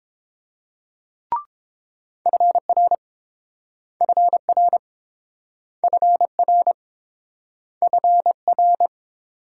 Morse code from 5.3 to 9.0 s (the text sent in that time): FR FR